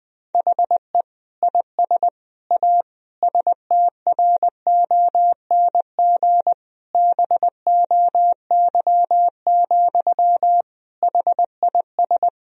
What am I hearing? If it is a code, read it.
Morse code, heard as a keyed tone, about 21 wpm, 700 Hz. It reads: HE IS A STRONG BOY, HIS
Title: Sentences from Top 1000 Words - Review Part 3 - 20wpm